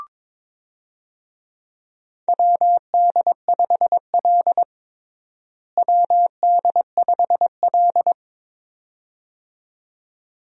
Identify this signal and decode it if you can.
Morse code, heard as a keyed tone, about 22 wpm, 700 Hz. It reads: WD5L WD5L
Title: USA Call Signs - 4 Characters - Repeated 2x - 22wpm